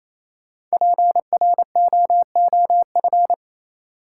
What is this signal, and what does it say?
Morse code, heard as a keyed tone, about 28 wpm, 700 Hz. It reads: PROOF
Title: Five Letter Words - 28wpm